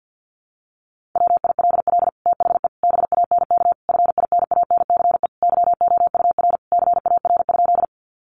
Morse code, heard as a keyed tone, about 50 wpm, 700 Hz. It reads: WILL THE BANK FINANCE YOUR CAR?
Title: Sentences from Top 1500 Words 50wpm